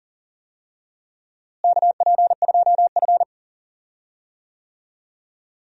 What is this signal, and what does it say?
Morse code, heard as a keyed tone, about 40 wpm, 700 Hz. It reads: KP2F
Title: DXCC-100 Call Signs - 4 Characters - Repeated 2x - 40wpm